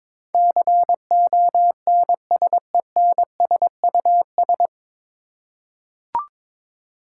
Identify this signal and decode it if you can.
Morse code, heard as a keyed tone, about 22 wpm, 700 Hz. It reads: CONSENSUS
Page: Morse Code Ninja practice 22wpm